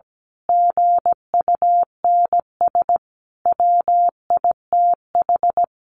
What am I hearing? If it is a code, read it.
Morse code, heard as a keyed tone, about 17 wpm, 700 Hz. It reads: GUNS WITH
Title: A Sound Of Thunder 17wpm